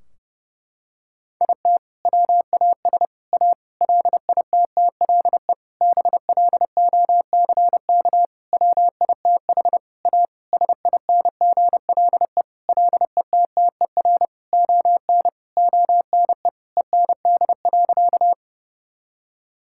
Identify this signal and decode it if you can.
Morse code, heard as a keyed tone, about 32 wpm, 700 Hz. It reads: IT WAS A LITTLE BLOCK WITH A SINGLE LETTER ON ONE END.
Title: Sentences from Top 1600 Words 30wpm